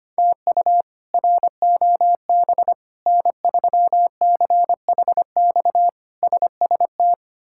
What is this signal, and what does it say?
Morse code, heard as a keyed tone, about 25 wpm, 700 Hz. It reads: TU ROB N3CHX SST
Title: K1USN SST QSO Practice - 25wpm